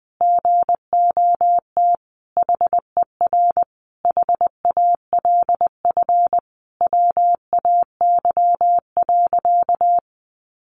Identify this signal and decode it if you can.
Morse code, heard as a keyed tone, about 20 wpm, 700 Hz. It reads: GOT HER HALF WAY.
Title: Sentences from Top 400 Words 20wpm